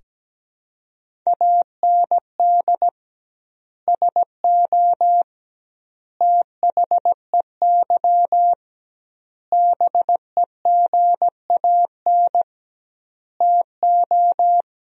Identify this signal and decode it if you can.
Morse code, heard as a keyed tone, about 17 wpm, 700 Hz. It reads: AND SO THEY BEGAN TO
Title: 2x Word-Spacing - Sentences from Top 400 Words - 17wpm